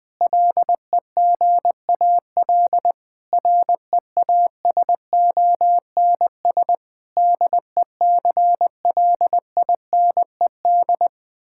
Morse code, heard as a keyed tone, about 20 wpm, 700 Hz. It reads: LEGAL REASONS DECLINED
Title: Sentences with 8-Word Phrases - 20wpm